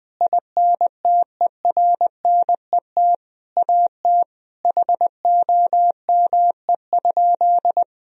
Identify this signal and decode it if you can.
Morse code, heard as a keyed tone, about 20 wpm, 700 Hz. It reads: INTERNET AT HOME?
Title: Sentences from Top 1000 Words - Review Part 4 - 20wpm